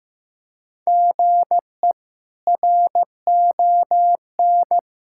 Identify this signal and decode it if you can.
Morse code, heard as a keyed tone, about 15 wpm, 700 Hz. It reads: GE RON
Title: K1USN SST QSO Practice - 15wpm